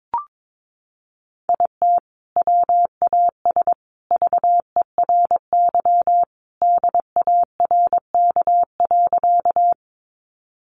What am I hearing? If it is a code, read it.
Morse code, heard as a keyed tone, about 22 wpm, 700 Hz. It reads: IT WAS VERY DARK.
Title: Sentences from Top 500 Words 22wpm